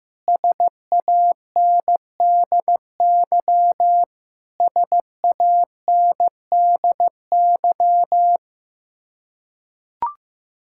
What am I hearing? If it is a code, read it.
Morse code, heard as a keyed tone, about 15 wpm, 700 Hz. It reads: SANDY SANDY
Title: Practice QSO - Exchange 1 - Beginning - 15wpm